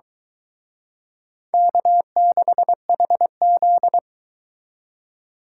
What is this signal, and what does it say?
Morse code, heard as a keyed tone, about 24 wpm, 700 Hz. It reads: K6HZ